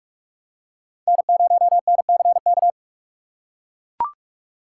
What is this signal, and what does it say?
Morse code, heard as a keyed tone, about 44 wpm, 700 Hz. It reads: N0NCK
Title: USA Call Signs - 5 Characters - 6x Character Spacing - 45wpm